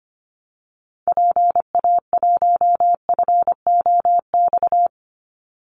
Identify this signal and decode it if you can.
Morse code, heard as a keyed tone, about 25 wpm, 700 Hz. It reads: PA1FOX
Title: Call Signs - 25wpm